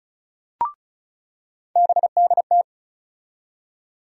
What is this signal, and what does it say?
Morse code, heard as a keyed tone, about 36 wpm, 700 Hz. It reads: BDT
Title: USA Call Sign Postfixes - 3 Letters 35wpm